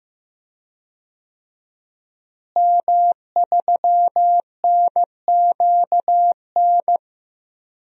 Morse code, heard as a keyed tone, about 15 wpm, 700 Hz. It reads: M3NQN
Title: DXCC-100 Call Signs - 5 Characters - Repeated 2x - 15wpm